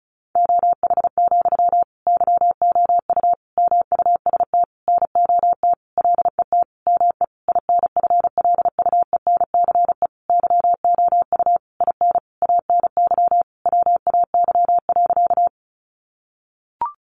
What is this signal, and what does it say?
Morse code, heard as a keyed tone, about 34 wpm, 700 Hz. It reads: OH, YOU MUST NOT LET ME INFLUENCE YOU IN ANY WAY.
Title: Sentences from Top 1100 Words 35wpm